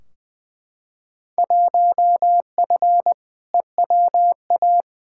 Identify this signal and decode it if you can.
Morse code, heard as a keyed tone, about 21 wpm, 700 Hz. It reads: 1F EWA
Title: Field Day Exchanges Encore - 20wpm